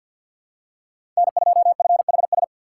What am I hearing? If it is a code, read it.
Morse code, heard as a keyed tone, about 52 wpm, 700 Hz. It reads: N2FHS
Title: USA Call Signs - 2x Character Spacing - 50wpm